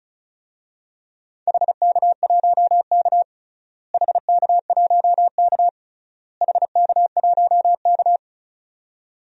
Morse code, heard as a keyed tone, about 36 wpm, 700 Hz. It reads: HK1K HK1K HK1K
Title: DXCC-100 Call Signs - 4 Characters - Repeated 3x - 35wpm